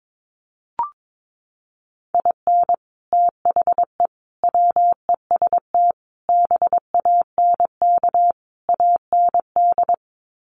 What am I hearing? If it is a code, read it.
Morse code, heard as a keyed tone, about 22 wpm, 700 Hz. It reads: IN THE WEST BANK AND
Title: Sets of 5 Words Encore2 22wpm